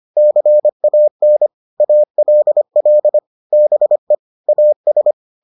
Morse code, heard as a keyed tone, about 25 wpm, 600 Hz. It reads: CAN ALL BE AS